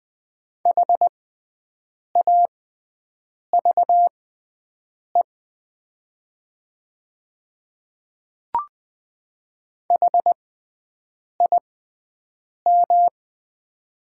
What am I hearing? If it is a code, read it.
Morse code, heard as a keyed tone, about 20 wpm, spaced as for 7 wpm, 700 Hz. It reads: HAVE HIM
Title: 6x Character Spacing - Top 100 Words - 20wpm